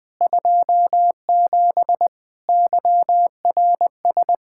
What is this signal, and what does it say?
Morse code, heard as a keyed tone, about 21 wpm, 700 Hz. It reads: 27 YRS
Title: Practice QSO - Exchange 3 - More Info - 20wpm